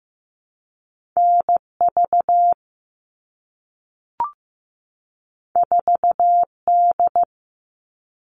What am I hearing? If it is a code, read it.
Morse code, heard as a keyed tone, about 15 wpm, 700 Hz. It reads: NV 4D
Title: All Cut Numbers - 2-Digit 15wpm